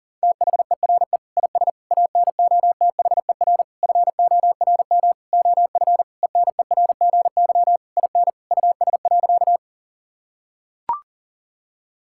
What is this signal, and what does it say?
Morse code, heard as a keyed tone, about 37 wpm, 700 Hz. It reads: THERE IS ANOTHER FORM OF ENERGY IN US.
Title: Sentences from Top 1000 Words - Review Part 6 - 40wpm